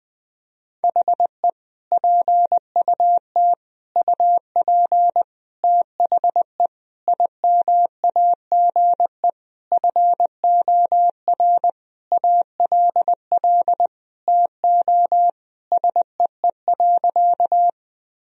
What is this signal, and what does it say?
Morse code, heard as a keyed tone, about 21 wpm, 700 Hz. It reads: HE PUT UP THE IMAGE FOR ALL TO SEE.
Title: Sentences from Top 1000 Words - Review Part 4 - 20wpm